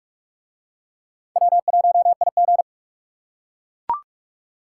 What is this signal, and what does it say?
Morse code, heard as a keyed tone, about 44 wpm, 700 Hz. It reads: W1IG